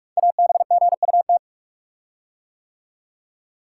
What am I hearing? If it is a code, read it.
Morse code, heard as a keyed tone, about 44 wpm, 690 Hz. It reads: ABGUT
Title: Code Groups - 5 Letters - 45wpm